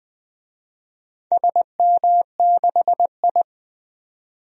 Morse code, heard as a keyed tone, about 20 wpm, 700 Hz. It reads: SM6I
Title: DXCC-100 Call Signs - 4 Characters - 20wpm